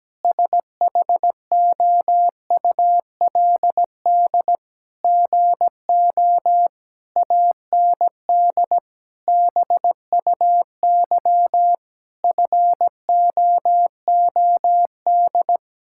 Morse code, heard as a keyed tone, about 17 wpm, 700 Hz. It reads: SHOULD GO AND BUY FOOD
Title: Sentences from Top 1600 Words 17wpm